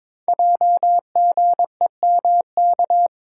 Morse code, heard as a keyed tone, about 22 wpm, 700 Hz. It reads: JGEMK
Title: Code Groups - 5 Letters - 22wpm